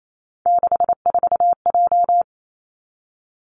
Morse code, heard as a keyed tone, about 28 wpm, 700 Hz. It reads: 64J
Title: Three Challenging Letter-Number Combinations 28wpm